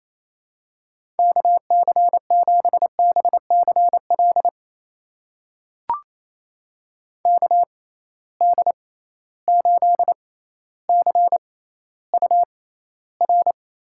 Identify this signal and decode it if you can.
Morse code, heard as a keyed tone, about 28 wpm, spaced as for 14 wpm, 700 Hz. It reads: KC7BCL KD8CUR
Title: USA Call Signs - 6 Characters - 6x Character Spacing - 28wpm